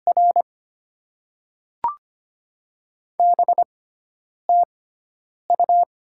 Morse code, heard as a keyed tone, about 25 wpm, spaced as for 9 wpm, 700 Hz. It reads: R BTU
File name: Top 20 QSO Elements - 6x Character Spacing - 25wpm